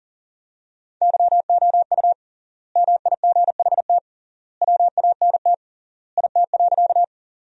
Morse code, heard as a keyed tone, about 40 wpm, 700 Hz. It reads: YOU MIGHT WANT IT.
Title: 3x Word-Spacing - Sentences from Top 300 Words - 40wpm